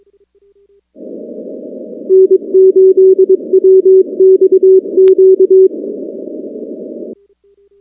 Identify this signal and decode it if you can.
Morse code, heard as a keyed tone, about 22 wpm, 380 Hz. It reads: N8WXQ
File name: contests / 2019-cw-wpx-kt5j-k5pi- -2330 0-19